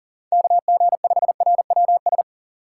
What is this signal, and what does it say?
Morse code, heard as a keyed tone, about 49 wpm, 700 Hz. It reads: KG5RWS
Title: Speed Racing - USA 2x3 Call Signs 40wpm